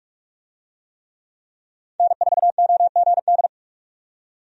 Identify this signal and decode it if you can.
Morse code, heard as a keyed tone, about 46 wpm, 700 Hz. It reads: N4OGD
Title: USA Call Signs - 5 Characters - 45wpm